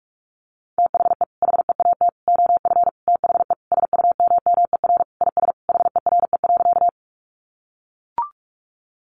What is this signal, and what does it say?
Morse code, heard as a keyed tone, about 44 wpm, 700 Hz. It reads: THE HEAT OF THE SUMMER IS HERE.
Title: Sentences from Top 1100 Words 45wpm